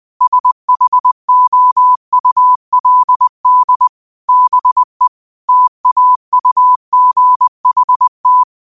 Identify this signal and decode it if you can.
Morse code, heard as a keyed tone, about 20 wpm, 1000 Hz. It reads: SHOULD BE TAUGHT
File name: qotdfiles / 20wpm